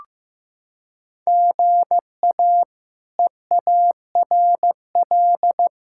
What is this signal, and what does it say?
Morse code, heard as a keyed tone, about 16 wpm, 700 Hz. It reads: GA EARL
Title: K1USN SST QSO Practice - 15wpm